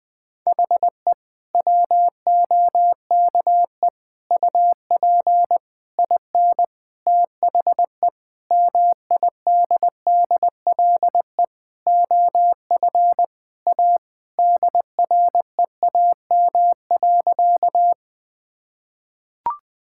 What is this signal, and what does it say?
Morse code, heard as a keyed tone, about 20 wpm, 700 Hz. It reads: HE WOKE UP IN THE MIDDLE OF A DREAM.
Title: Sentences with 4-Word Phrases - Spoken Beginning - 20wpm